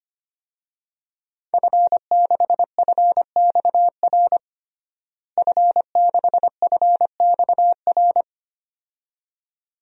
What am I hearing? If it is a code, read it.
Morse code, heard as a keyed tone, about 25 wpm, 700 Hz. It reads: F6FXR F6FXR
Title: DXCC-100 Call Signs - 5 Characters - Repeated 2x - 25wpm